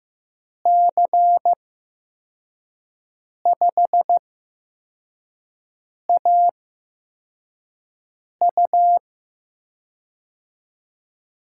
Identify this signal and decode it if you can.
Morse code, heard as a keyed tone, about 15 wpm, spaced as for 4 wpm, 700 Hz. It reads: C5AU
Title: USA Call Signs - 5 Characters - 8x Character Spacing - 15wpm